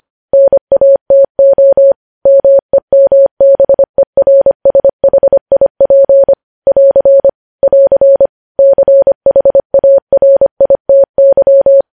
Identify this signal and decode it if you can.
Morse code, heard as a keyed tone, about 25 wpm, 550 Hz. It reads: NATO MEMBERSHIP <AR> <AR> CHARITY
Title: News Headlines at 25